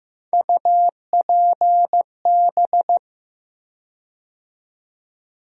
Morse code, heard as a keyed tone, about 15 wpm, 700 Hz. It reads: UPB
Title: USA Call Sign Postfixes - 3 Letters 15wpm